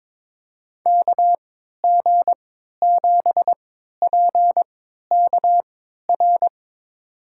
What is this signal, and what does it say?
Morse code, heard as a keyed tone, about 22 wpm, spaced as for 12 wpm, 700 Hz. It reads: KG7PKR
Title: USA Call Signs - 6 Characters - 3x Character Spacing - 22wpm